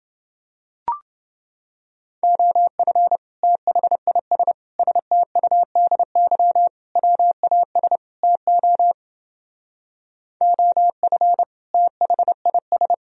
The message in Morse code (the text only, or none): OF THIS STUDY WAS TO OF THIS